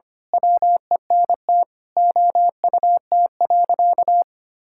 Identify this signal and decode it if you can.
Morse code, heard as a keyed tone, about 25 wpm, 700 Hz. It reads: WENT OUT.
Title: Sentences from Top 1600 Words 25wpm